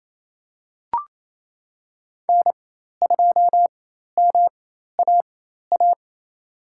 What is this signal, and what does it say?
Morse code, heard as a keyed tone, about 28 wpm, spaced as for 13 wpm, 700 Hz. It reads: N2MAA
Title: USA Call Signs - 4x Character Spacing - 28wpm